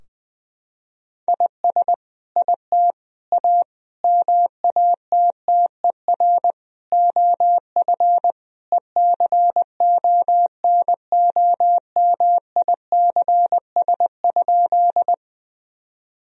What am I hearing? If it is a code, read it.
Morse code, heard as a keyed tone, about 21 wpm, 700 Hz. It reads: IS IT A MATTER OF ECONOMICS?
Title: Sentences from Top 1000 Words - Review Part 3 - 20wpm